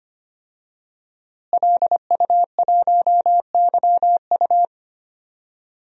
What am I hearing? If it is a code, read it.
Morse code, heard as a keyed tone, about 25 wpm, 700 Hz. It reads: LU1YU
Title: DXCC-100 Call Signs - 5 Characters - Repeated 2x - 25wpm